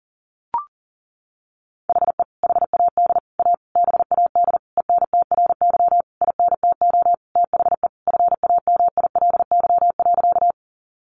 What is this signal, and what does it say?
Morse code, heard as a keyed tone, about 40 wpm, 700 Hz. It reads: HE HAD A BAD ENTRY INTO THE FAMILY.
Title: Sentences from Top 1500 Words 40wpm